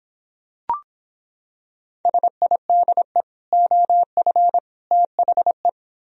Morse code, heard as a keyed tone, about 26 wpm, 700 Hz. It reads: SIDE OF THE